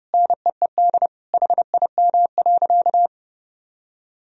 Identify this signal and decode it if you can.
Morse code, heard as a keyed tone, about 30 wpm, 700 Hz. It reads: NEED HIM.
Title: Sentences from Top 1000 Words - Review Part 1 - 30wpm